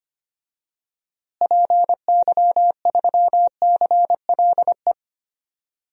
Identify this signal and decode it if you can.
Morse code, heard as a keyed tone, about 25 wpm, 700 Hz. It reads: PY3CLE
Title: DXCC-100 Call Signs - 6 Characters - Repeated 2x - 25wpm